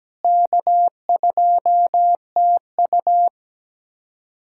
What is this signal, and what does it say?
Morse code, heard as a keyed tone, about 17 wpm, 700 Hz. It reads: K2TU
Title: USA Callsign DE Callsign - 4 Characters - 17wpm